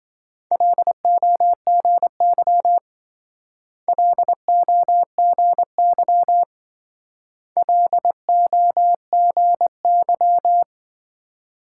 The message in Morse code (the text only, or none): LOGY LOGY LOGY